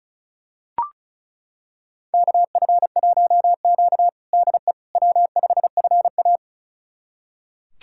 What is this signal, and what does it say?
Morse code, heard as a keyed tone, about 36 wpm, 690 Hz. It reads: KF1Q DE W5FA
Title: USA Callsign DE Callsign - 4 Characters - 35wpm